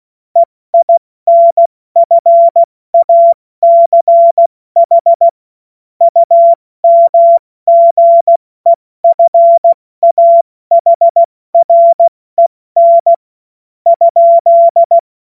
Morse code, heard as a keyed tone, about 16 wpm, spaced as for 14 wpm, 680 Hz. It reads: EINFACH UMGEFAHREN ?
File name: Witz-12-August-2025-70-F80